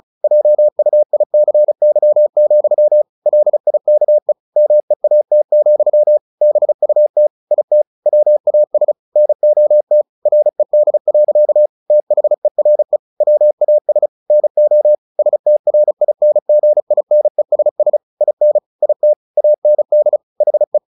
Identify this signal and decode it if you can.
Morse code, heard as a keyed tone, about 34 wpm, 600 Hz. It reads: JUICY, LIKE MEAT, BUT IT WAS NOT RED. THERE WAS NO STRINGINESS IN IT AND HE
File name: the old man and the sea 2 35wpm